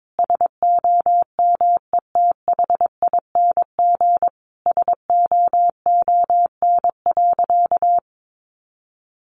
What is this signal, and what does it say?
Morse code, heard as a keyed tone, about 22 wpm, 700 Hz. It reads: SOMETHING SOON.
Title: Sentences from Top 1500 Words 22wpm